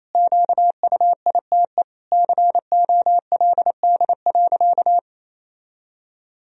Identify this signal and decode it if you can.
Morse code, heard as a keyed tone, about 28 wpm, 700 Hz. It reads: QUITE COLD.
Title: Sentences from Top 1000 Words - Review Part 4 - 28wpm